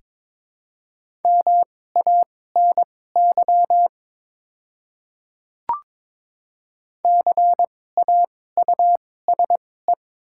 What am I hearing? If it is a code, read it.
Morse code, heard as a keyed tone, about 21 wpm, spaced as for 16 wpm, 700 Hz. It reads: MANY CAUSE
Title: 2x Character Spacing - Top 200 Words - 22wpm